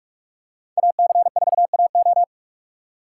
Morse code, heard as a keyed tone, about 46 wpm, 700 Hz. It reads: AC4AO
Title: USA Call Signs - 5 Characters - 2x Character Spacing - 45wpm